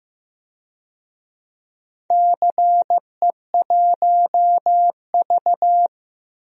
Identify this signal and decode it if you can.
Morse code, heard as a keyed tone, about 15 wpm, 700 Hz. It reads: CE1V
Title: DXCC-100 Call Signs - 4 Characters - Repeated 2x - 15wpm